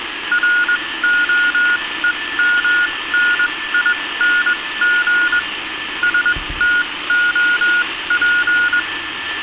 Morse code, heard as a keyed tone, about 21 wpm, spaced as for 17 wpm, 1440 Hz. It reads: ROEMNING STOP